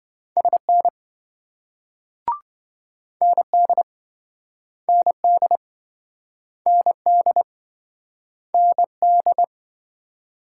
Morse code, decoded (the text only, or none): SN ND ND ND ND